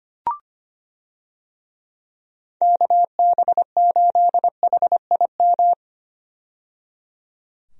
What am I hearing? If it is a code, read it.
Morse code, heard as a keyed tone, about 25 wpm, 700 Hz. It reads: KB8HIM
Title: USA Call Signs - 6 Characters - 25wpm